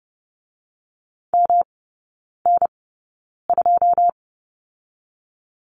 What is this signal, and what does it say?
Morse code, heard as a keyed tone, about 30 wpm, spaced as for 9 wpm, 700 Hz. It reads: MN2
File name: Lesson 18e - Letter m - Three Characters 30wpm